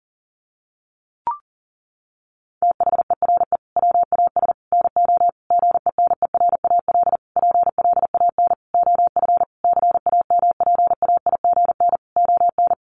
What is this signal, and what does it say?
Morse code, heard as a keyed tone, about 40 wpm, 700 Hz. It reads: THERE WAS NO GENERAL PLAN OF CAMPAIGN ON
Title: Sentences from Top 1200 Words 40wpm